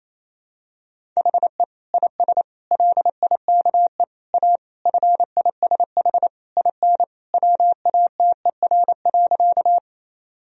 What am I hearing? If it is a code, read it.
Morse code, heard as a keyed tone, about 28 wpm, 700 Hz. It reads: HE IS LIKE A FISH IN WATER.
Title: Sentences from Top 1000 Words - Review Part 1 - 28wpm